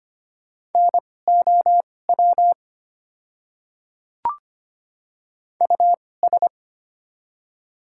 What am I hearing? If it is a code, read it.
Morse code, heard as a keyed tone, about 25 wpm, spaced as for 18 wpm, 690 Hz. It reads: NOW US
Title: 2x Character Spacing - Top 200 Words - 25wpm